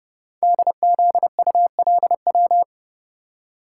Morse code, heard as a keyed tone, about 30 wpm, 690 Hz. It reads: DZULW
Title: Code Groups - 5 Letters - 30wpm